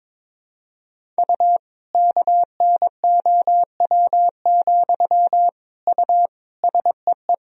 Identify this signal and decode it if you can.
Morse code, heard as a keyed tone, about 22 wpm, 700 Hz. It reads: U KNOW, U SEE
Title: Sentences from Top 200 Words and Abbreviations 22wpm